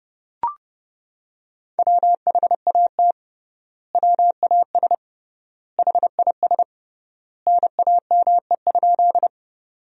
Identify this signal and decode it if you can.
Morse code, heard as a keyed tone, about 30 wpm, 700 Hz. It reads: WHAT WAS HIS NAME?
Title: 3x Word-Spacing - Sentences from Top 200 Words - 30wpm